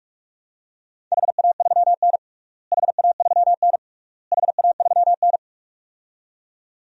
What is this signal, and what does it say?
Morse code, heard as a keyed tone, about 46 wpm, 700 Hz. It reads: HA3N HA3N HA3N
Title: DXCC-100 Call Signs - 4 Characters - Repeated 3x - 45wpm